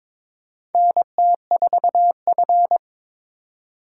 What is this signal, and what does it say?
Morse code, heard as a keyed tone, about 21 wpm, 700 Hz. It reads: NT4F
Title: USA Call Signs - 4 Characters - 2x Character Spacing - 22wpm